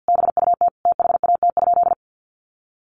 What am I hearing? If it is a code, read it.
Morse code, heard as a keyed tone, about 48 wpm, 700 Hz. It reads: BUT THAT?
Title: Sentences from Top 1400 Words 50wpm